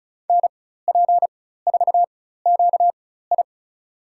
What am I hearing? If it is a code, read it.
Morse code, heard as a keyed tone, about 36 wpm, spaced as for 16 wpm, 700 Hz. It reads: NP4QI
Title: USA Call Signs - 4x Character Spacing - 35wpm